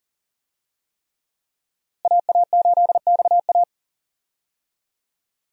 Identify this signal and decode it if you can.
Morse code, heard as a keyed tone, about 40 wpm, 700 Hz. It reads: AA8XA
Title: USA Call Signs - 5 Characters - Repeated 3x - 40wpm